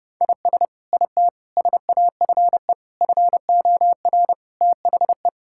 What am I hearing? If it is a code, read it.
Morse code, heard as a keyed tone, about 30 wpm, 700 Hz. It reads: IS IT SAFE FOR THE
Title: Sentences from Top 1600 Words 30wpm